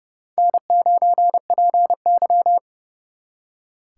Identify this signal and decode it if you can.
Morse code, heard as a keyed tone, about 39 wpm, 700 Hz. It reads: N9PY